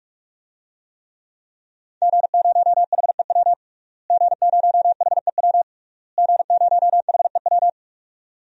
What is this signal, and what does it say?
Morse code, heard as a keyed tone, about 46 wpm, 700 Hz. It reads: G0HEW G0HEW G0HEW
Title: DXCC-100 Call Signs - 5 Characters - Repeated 3x - 45wpm